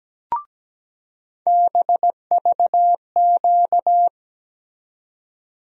Morse code, heard as a keyed tone, about 17 wpm, 700 Hz. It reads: BVQ
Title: USA Call Sign Postfixes - 3 Letters 17wpm